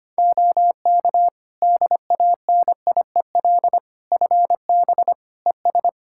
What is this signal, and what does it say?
Morse code, heard as a keyed tone, about 25 wpm, 700 Hz. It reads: OK DANIEL FB ES
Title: K1USN SST QSO Practice 25wpm